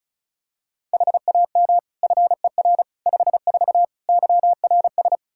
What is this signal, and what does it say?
Morse code, heard as a keyed tone, about 36 wpm, 700 Hz. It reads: HAM FER 54 YRS
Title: K1USN SST QSO Practice 35wpm